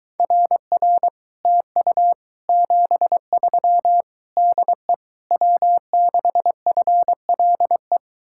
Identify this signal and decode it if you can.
Morse code, heard as a keyed tone, about 23 wpm, 700 Hz. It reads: RR TU 73 DE W6FLE